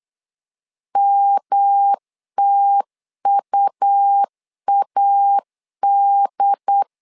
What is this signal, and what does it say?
Morse code, heard as a keyed tone, about 8 wpm, 800 Hz. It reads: MTUAD
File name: LEZIONI / 3 MWRO / 3 M 8wpm 2300